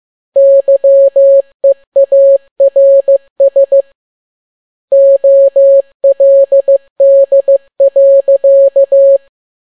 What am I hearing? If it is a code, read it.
Morse code, heard as a keyed tone, about 15 wpm, 550 Hz. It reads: YEARS OLD.
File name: n6tt p1 15C13